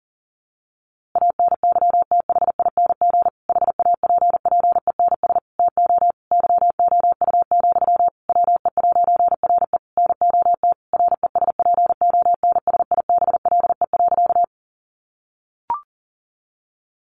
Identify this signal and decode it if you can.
Morse code, heard as a keyed tone, about 40 wpm, 700 Hz. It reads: ANYTHING HAPPENS TO YOU, WE'RE NOT RESPONSIBLE.
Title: A Sound Of Thunder 40wpm